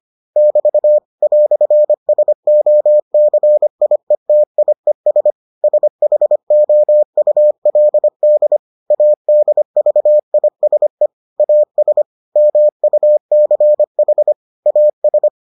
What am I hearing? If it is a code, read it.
Morse code, heard as a keyed tone, about 25 wpm, 600 Hz. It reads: <BT> "SOCIETIES SHOULD ADVISE AS MUCH AS